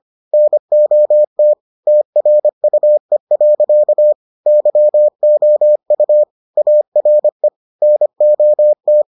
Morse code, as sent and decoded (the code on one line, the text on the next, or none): -. --- - / - .-. ..- . .-.-.- / -.-- --- ..- / .- .-. / -. --- -
NOT TRUE. YOU ARE NOT